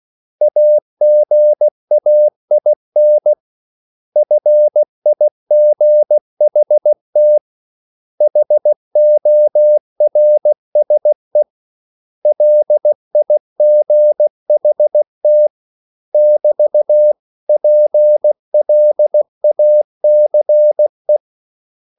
Morse code, heard as a keyed tone, about 16 wpm, 600 Hz. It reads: AGAIN FIGHT HORSE LIGHT <BT> PLACE
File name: CW / Practice / 500MostCommonEnglishWords16